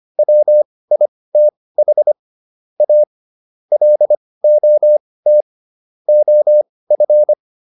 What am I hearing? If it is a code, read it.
Morse code, heard as a keyed tone, about 25 wpm, spaced as for 18 wpm, 600 Hz. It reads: WITH A LOT OF